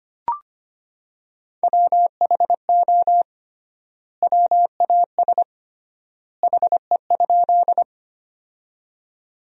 Morse code, heard as a keyed tone, about 25 wpm, 700 Hz. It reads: WHO WAS HE?
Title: 3x Word-Spacing - Sentences from Top 100 Words - 25wpm